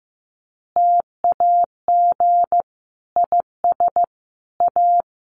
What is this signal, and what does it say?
Morse code, heard as a keyed tone, about 14 wpm, 700 Hz. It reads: TAG IS A
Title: Additional Easy Phrases 15wpm